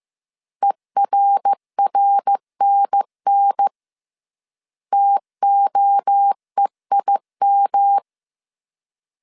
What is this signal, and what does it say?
Morse code, heard as a keyed tone, about 15 wpm, 800 Hz. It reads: ERRNN TOEIM